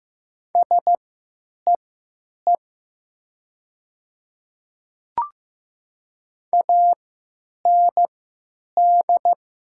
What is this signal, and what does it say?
Morse code, heard as a keyed tone, about 15 wpm, spaced as for 9 wpm, 700 Hz. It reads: SEE AND
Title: 3x Character Spacing - Top 200 Words - 15wpm